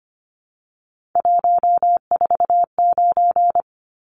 Morse code, heard as a keyed tone, about 25 wpm, 700 Hz. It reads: 149
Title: Numbers - 3-Digit 25wpm